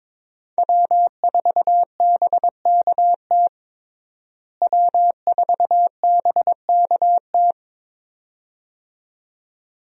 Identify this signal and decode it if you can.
Morse code, heard as a keyed tone, about 22 wpm, 700 Hz. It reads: W4BKT W4BKT
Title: USA Call Signs - 5 Characters - Repeated 2x - 22wpm